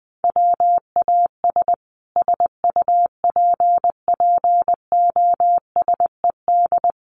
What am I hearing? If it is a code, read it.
Morse code, heard as a keyed tone, about 20 wpm, 700 Hz. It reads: WAS SUPPOSED
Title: Sets of 5 Words Encore 20wpm